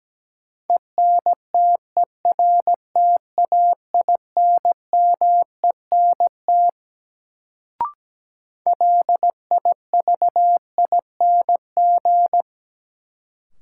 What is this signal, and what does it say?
Morse code, heard as a keyed tone, about 17 wpm, 700 Hz. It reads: ENTERTAINMENT LIVING
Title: Sentences from Top 1000 Words - Review Part 6 - New Word Drill - 17wpm